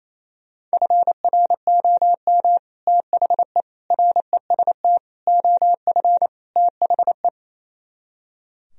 Morse code, heard as a keyed tone, about 35 wpm, 700 Hz. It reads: FROM THE REST OF THE